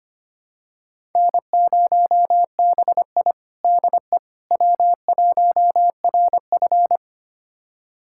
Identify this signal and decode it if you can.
Morse code, heard as a keyed tone, about 25 wpm, 700 Hz. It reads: N0BI DE W1RF
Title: USA Callsign DE Callsign - 4 Characters - 25wpm